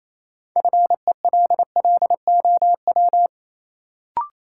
Morse code, heard as a keyed tone, about 28 wpm, 700 Hz. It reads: FELLOW
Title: Sets of 5 Words Encore - New Word Drill - 28wpm